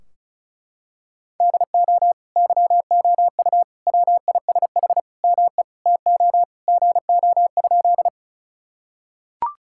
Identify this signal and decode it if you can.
Morse code, heard as a keyed tone, about 34 wpm, 700 Hz. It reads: DO YOU WISH ME TO GO?
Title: Sentences from Top 1600 Words 35wpm